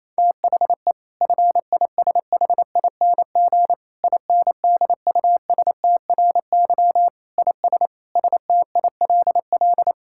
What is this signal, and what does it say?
Morse code, heard as a keyed tone, about 28 wpm, 700 Hz. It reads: THE FISHING INDUSTRY IS STILL
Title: Sentences from Top 1000 Words - Review Part 6 - 28wpm